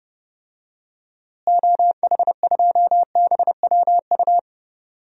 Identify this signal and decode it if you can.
Morse code, heard as a keyed tone, about 30 wpm, 700 Hz. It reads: OH2BWU
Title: DXCC-100 Call Signs - 6 Characters - Repeated 2x - 30wpm